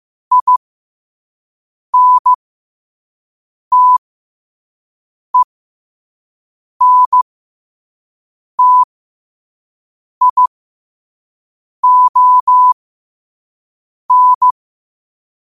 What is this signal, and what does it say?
Morse code, heard as a keyed tone, about 15 wpm, spaced as for 5 wpm, 1000 Hz. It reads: INTENTION
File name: qotdfiles / 5wpm